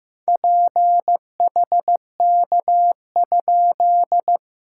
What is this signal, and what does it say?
Morse code, heard as a keyed tone, about 15 wpm, 690 Hz. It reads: PHK?
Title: SOTA QSO Practice 15wpm